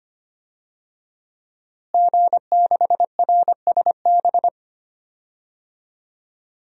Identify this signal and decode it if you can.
Morse code, heard as a keyed tone, about 25 wpm, 700 Hz. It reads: G6RSB
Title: DXCC-100 Call Signs - 5 Characters - Repeated 3x - 25wpm